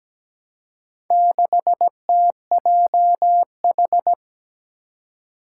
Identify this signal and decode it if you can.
Morse code, heard as a keyed tone, about 17 wpm, 700 Hz. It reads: 6TJH
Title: Four Challenging Letter-Number Combinations 17wpm